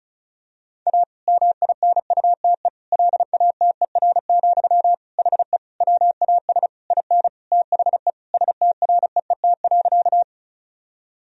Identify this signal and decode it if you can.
Morse code, heard as a keyed tone, about 34 wpm, 700 Hz. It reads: A MINUTE LATER, HE WAS IN THE STREET.
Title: Sentences from Top 1000 Words - Review Part 4 - 35wpm